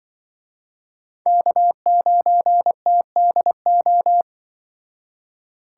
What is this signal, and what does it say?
Morse code, heard as a keyed tone, about 23 wpm, 700 Hz. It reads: K9TDO
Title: Speed Racing - USA 1x3 Call Signs 20wpm